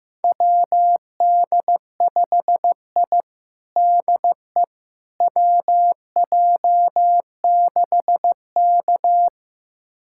Morse code, heard as a keyed tone, about 16 wpm, 700 Hz. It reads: WD5I DE WJ6K
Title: USA Callsign DE Callsign - 4 Characters - 15wpm